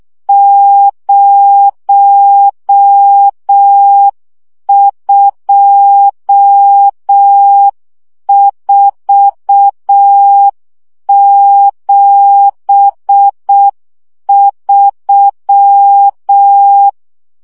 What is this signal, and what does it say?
Morse code, heard as a keyed tone, about 6 wpm, 800 Hz. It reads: 02473
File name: morze vizsga 2011-3 5